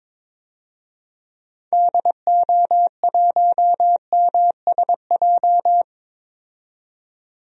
Morse code, heard as a keyed tone, about 22 wpm, 700 Hz. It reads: DO1MSJ
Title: DXCC-100 Call Signs - 6 Characters - 22wpm